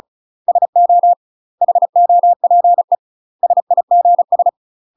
Morse code, heard as a keyed tone, about 34 wpm, 700 Hz. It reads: SO HOPE SIGS
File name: UK MASTER QSOs TRAINING / 35WPM / UK MASTER QSOs 0015